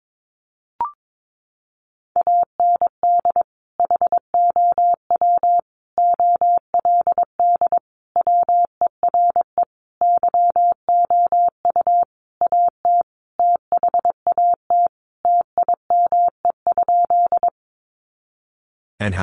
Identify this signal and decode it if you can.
Morse code, heard as a keyed tone, about 22 wpm, 700 Hz. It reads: AND HOW OLD WERE YOU AT THAT TIME?
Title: Sentences from Top 200 Words 22wpm